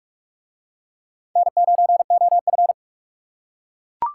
Morse code, heard as a keyed tone, about 40 wpm, 700 Hz. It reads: N9OF